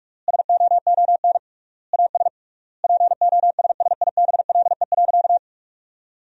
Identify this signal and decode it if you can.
Morse code, heard as a keyed tone, about 46 wpm, 700 Hz. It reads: SOON AS POSSIBLE.
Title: 3x Word-Spacing - Sentences from Top 500 Words - 45wpm